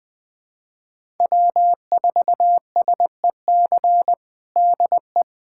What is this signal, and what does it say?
Morse code, heard as a keyed tone, about 21 wpm, 700 Hz. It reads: W4SEC DE
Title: USA Callsign DE Callsign - 5 Characters - 20wpm